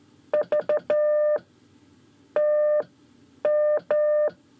Morse code, heard as a keyed tone, about 12 wpm, 600 Hz. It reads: V T M
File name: 5 Lezione CW 1